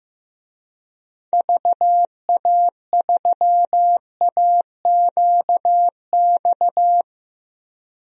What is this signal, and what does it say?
Morse code, heard as a keyed tone, about 15 wpm, 700 Hz. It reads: VA3AQX
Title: DXCC-100 Call Signs - 6 Characters - 15wpm